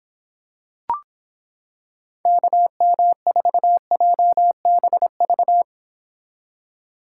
Morse code, heard as a keyed tone, about 26 wpm, 700 Hz. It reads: KM4JBV